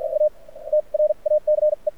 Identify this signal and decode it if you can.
Morse code, heard as a keyed tone, about 52 wpm, spaced as for 39 wpm, 590 Hz. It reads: A TRACE
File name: archived / CRSnet